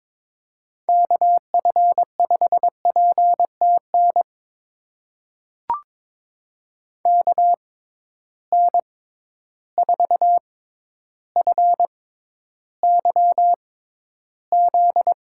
Morse code, heard as a keyed tone, about 22 wpm, spaced as for 11 wpm, 700 Hz. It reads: KF5PTN KN4FYZ
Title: USA Call Signs - 6 Characters - 6x Character Spacing - 22wpm